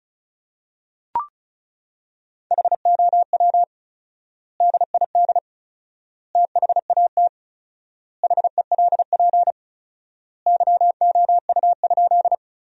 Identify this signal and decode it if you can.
Morse code, heard as a keyed tone, about 34 wpm, 700 Hz. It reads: HOW DID THAT HELP YOU?
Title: 4x Word-Spacing - Sentences from Top 200 Words - 35wpm